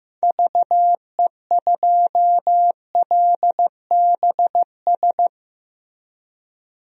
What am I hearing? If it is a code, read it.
Morse code, heard as a keyed tone, about 16 wpm, 700 Hz. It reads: VE2LBS
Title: DXCC-100 Call Signs - 6 Characters - Repeated 2x - 15wpm